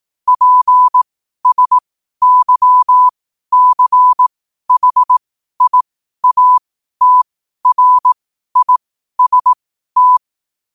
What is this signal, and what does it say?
Morse code, heard as a keyed tone, about 18 wpm, spaced as for 12 wpm, 1000 Hz. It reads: PSYCHIATRIST
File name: qotdfiles / 12wpm